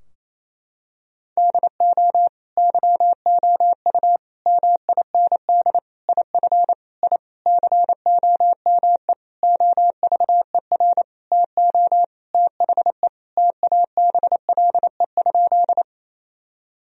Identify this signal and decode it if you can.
Morse code, heard as a keyed tone, about 28 wpm, 700 Hz. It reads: DO YOU MIND IF I COME OVER TO THE TABLE?
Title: Sentences from Top 1000 Words - Review Part 2 - 28wpm